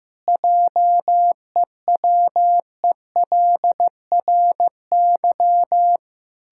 Morse code, heard as a keyed tone, about 15 wpm, 700 Hz. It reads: JEWELRY
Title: Pangram Challenge v2 - 15wpm